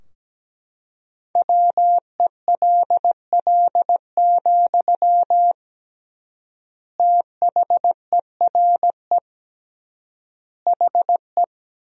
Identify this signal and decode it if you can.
Morse code, heard as a keyed tone, about 17 wpm, 700 Hz. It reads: WELL, THERE HE
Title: 3x Word-Spacing - Sentences from Top 500 Words - 17wpm